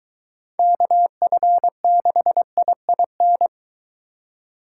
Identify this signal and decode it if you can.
Morse code, heard as a keyed tone, about 22 wpm, 700 Hz. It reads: KF6IIN